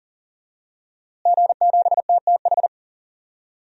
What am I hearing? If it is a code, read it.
Morse code, heard as a keyed tone, about 40 wpm, 700 Hz. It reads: G7TTH
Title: DXCC-100 Call Signs - 5 Characters - Repeated 3x - 40wpm